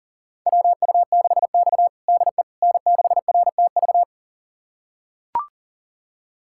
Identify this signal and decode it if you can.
Morse code, heard as a keyed tone, about 40 wpm, 700 Hz. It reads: WU6X DE N6RTV
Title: USA Callsign DE Callsign 40wpm